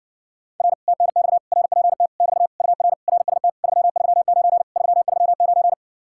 Morse code, heard as a keyed tone, about 58 wpm, 700 Hz. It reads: R TNX RPT <BT> UR RST 339 339